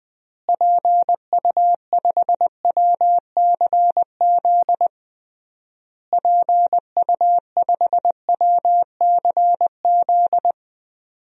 Morse code, heard as a keyed tone, about 21 wpm, 700 Hz. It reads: PU5WCZ PU5WCZ
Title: DXCC-100 Call Signs - 6 Characters - Repeated 3x - 20wpm